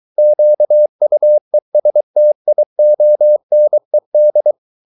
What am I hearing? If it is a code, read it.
Morse code, heard as a keyed tone, about 23 wpm, 600 Hz. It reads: QUESTIONED